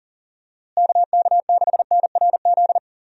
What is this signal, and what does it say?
Morse code, heard as a keyed tone, about 40 wpm, 700 Hz. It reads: KK6NRZ
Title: USA Call Signs - 6 Characters - Repeated 2x - 40wpm